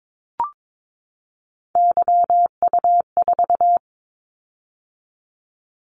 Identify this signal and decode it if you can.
Morse code, heard as a keyed tone, about 22 wpm, 700 Hz. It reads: YU4
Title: Callsign Prefixes for Top 25 DXCC - 22wpm